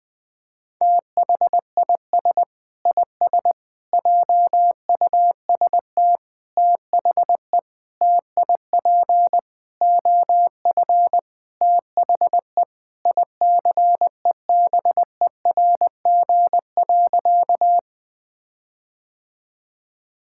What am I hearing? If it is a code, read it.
Morse code, heard as a keyed tone, about 20 wpm, 700 Hz. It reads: THIS IS JUST THE TIP OF THE ICEBERG.
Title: Sentences with 6-Word Phrases - 20wpm